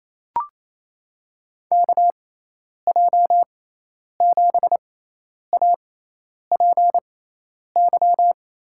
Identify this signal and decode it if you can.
Morse code, heard as a keyed tone, about 28 wpm, spaced as for 10 wpm, 700 Hz. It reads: KJ7APY
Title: USA Call Signs - 6x Character Spacing - 28wpm